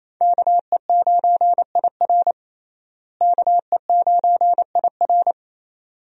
Morse code, heard as a keyed tone, about 28 wpm, 700 Hz. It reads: KE9IR KE9IR
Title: USA Call Signs - 5 Characters - Repeated 2x - 28wpm